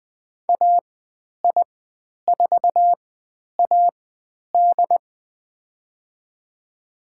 Morse code, heard as a keyed tone, about 20 wpm, spaced as for 10 wpm, 700 Hz. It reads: AI4AD
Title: 102 Lesson 17g - Letter u - Call Signs 20wpm